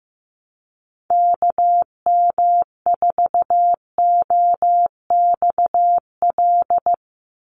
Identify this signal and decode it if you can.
Morse code, heard as a keyed tone, about 15 wpm, 700 Hz. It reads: KM4OXL
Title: USA Call Signs 15wpm